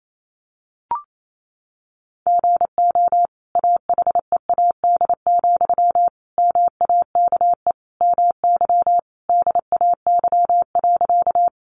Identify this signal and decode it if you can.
Morse code, heard as a keyed tone, about 28 wpm, 700 Hz. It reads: GO AHEAD, MAKE MY DAY.
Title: Sentences from Top 1100 Words 28wpm